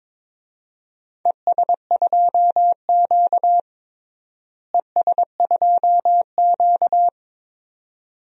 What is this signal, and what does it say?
Morse code, heard as a keyed tone, about 22 wpm, 700 Hz. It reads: ES2Q ES2Q
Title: DXCC-100 Call Signs - 4 Characters - Repeated 2x - 22wpm